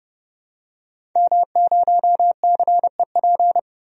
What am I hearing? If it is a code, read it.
Morse code, heard as a keyed tone, about 30 wpm, 700 Hz. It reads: M0CEP
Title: DXCC-100 Call Signs - 5 Characters - 30wpm